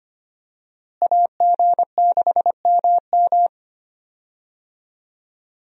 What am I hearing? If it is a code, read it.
Morse code, heard as a keyed tone, about 25 wpm, 700 Hz. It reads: AG6MM
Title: USA Call Signs - 5 Characters - Repeated 3x - 25wpm